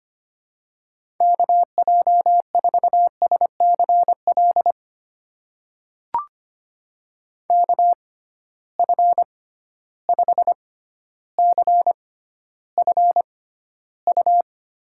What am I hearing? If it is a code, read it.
Morse code, heard as a keyed tone, about 25 wpm, spaced as for 13 wpm, 700 Hz. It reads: KJ4SCL KF5CFU
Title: USA Call Signs - 6 Characters - 6x Character Spacing - 25wpm